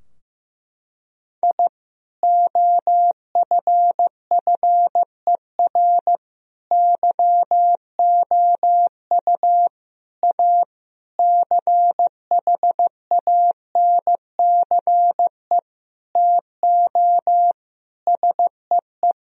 Morse code, heard as a keyed tone, about 14 wpm, 700 Hz. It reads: I OFFER YOU A CHANCE TO SEE
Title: Sentences from Top 1000 Words - Review Part 4 - 15wpm